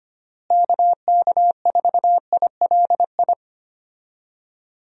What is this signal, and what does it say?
Morse code, heard as a keyed tone, about 25 wpm, 700 Hz. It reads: KK4ILI
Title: USA Call Signs - 6 Characters - Repeated 3x - 25wpm